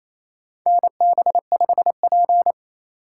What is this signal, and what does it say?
Morse code, heard as a keyed tone, about 28 wpm, 700 Hz. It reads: NB5P
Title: USA Call Signs - 4 Characters - 3x Character Spacing - 28wpm